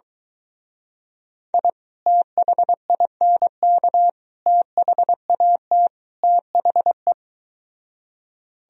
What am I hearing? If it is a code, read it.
Morse code, heard as a keyed tone, about 22 wpm, 700 Hz. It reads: I THINK THAT THE